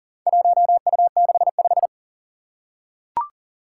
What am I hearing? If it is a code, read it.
Morse code, heard as a keyed tone, about 40 wpm, 700 Hz. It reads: 1U65
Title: Four Challenging Letter-Number Combinations 40wpm